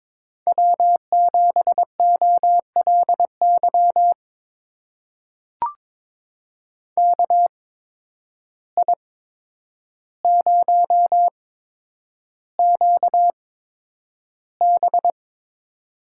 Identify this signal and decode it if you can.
Morse code, heard as a keyed tone, about 22 wpm, spaced as for 9 wpm, 700 Hz. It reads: W7OLY KI0QB
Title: USA Call Signs - 8x Character Spacing - 22wpm